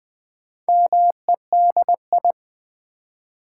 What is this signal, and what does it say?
Morse code, heard as a keyed tone, about 20 wpm, 700 Hz. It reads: MEDI
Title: Build A Word - 2 Letters At A Time - 8-Letter Words - 20wpm